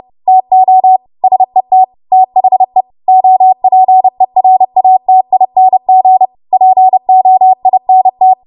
Morse code, heard as a keyed tone, about 30 wpm, 750 Hz. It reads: TO SET THE OPERATING POINT
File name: Archive / 30 WPM / 30WPM